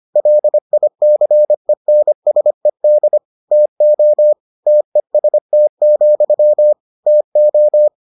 Morse code, heard as a keyed tone, about 25 wpm, 600 Hz. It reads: LICENSED TO TEST, TO